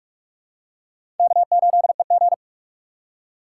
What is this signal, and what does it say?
Morse code, heard as a keyed tone, about 42 wpm, 700 Hz. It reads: K8EG
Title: USA Call Signs - 4 Characters - 6x Character Spacing - 45wpm